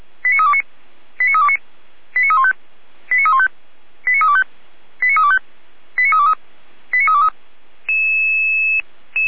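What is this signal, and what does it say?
Five-tone selective call (ZVEI), heard as quick runs of five tones, tones of about 70 ms. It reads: 89329 89329 89315 89315 89325 89325 89323 89323
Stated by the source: ZVEI five-tone selective call received on SDR.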